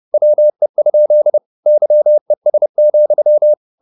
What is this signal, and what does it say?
Morse code, heard as a keyed tone, about 30 wpm, 600 Hz. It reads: WE? YES,
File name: the old man and the sea / the old man and the sea 1 30wpm